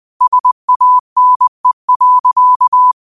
Morse code, heard as a keyed tone, about 20 wpm, 1000 Hz. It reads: SANE.
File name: qotdfiles / 20wpm